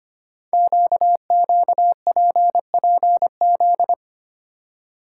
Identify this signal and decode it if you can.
Morse code, heard as a keyed tone, about 25 wpm, 700 Hz. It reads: QQPPZ
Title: Code Groups - 5 Letters - 25wpm